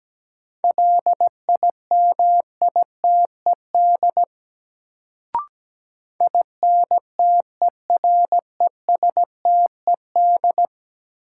Morse code, heard as a keyed tone, about 17 wpm, 700 Hz. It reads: LIMITED INTERESTED
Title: Sentences from Top 900 Words - New Word Drill - 17wpm